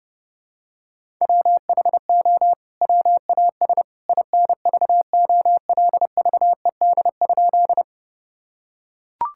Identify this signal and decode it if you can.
Morse code, heard as a keyed tone, about 30 wpm, 700 Hz. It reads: WHO WAS INVOLVED?
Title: Sentences from Top 1000 Words - Review Part 5 - 30wpm